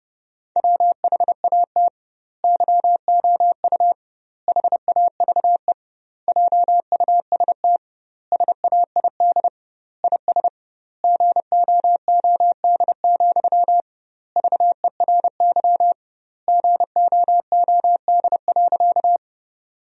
Morse code, heard as a keyed tone, about 30 wpm, 700 Hz. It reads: WHAT YOU HAVE JUST SAID IS GOOD, VERY GOOD.
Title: 2x Word-Spacing - Sentences from Top 200 Words - 30wpm